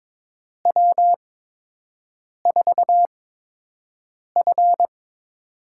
Morse code, heard as a keyed tone, about 22 wpm, spaced as for 6 wpm, 700 Hz. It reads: W4F
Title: USA Call Signs - 4 Characters - 8x Character Spacing - 22wpm